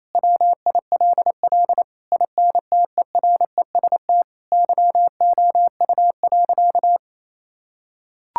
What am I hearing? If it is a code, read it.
Morse code, heard as a keyed tone, about 28 wpm, 700 Hz. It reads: WILL INTEREST YOU.
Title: Sentences from Top 1000 Words - Review Part 5 - 28wpm